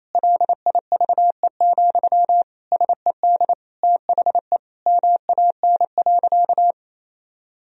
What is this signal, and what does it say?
Morse code, heard as a keyed tone, about 28 wpm, 700 Hz. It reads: LIVE, SED THE MAN.
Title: Sentences from Top 200 Words and Abbreviations 28wpm